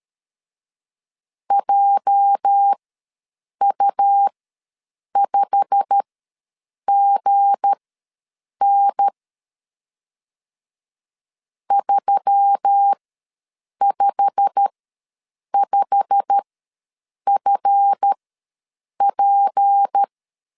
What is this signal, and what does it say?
Morse code, heard as a keyed tone, about 13 wpm, spaced as for 7 wpm, 800 Hz. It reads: JU5GN 355FP
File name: LEZIONI / 8 YJ45 / 8 YJ45 ws 12wpm 3146